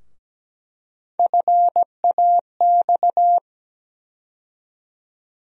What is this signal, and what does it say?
Morse code, heard as a keyed tone, about 17 wpm, 700 Hz. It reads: FAX
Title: USA Call Sign Postfixes - 3 Letters 17wpm